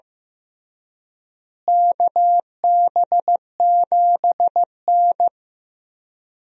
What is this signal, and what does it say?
Morse code, heard as a keyed tone, about 15 wpm, 700 Hz. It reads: KB7N
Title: USA Call Signs - 4 Characters - Repeated 3x - 15wpm